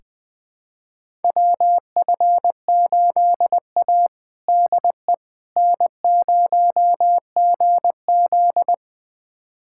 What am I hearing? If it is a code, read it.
Morse code, heard as a keyed tone, about 20 wpm, 700 Hz. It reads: WF8A DE N0GZ
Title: USA Callsign DE Callsign - 4 Characters - 20wpm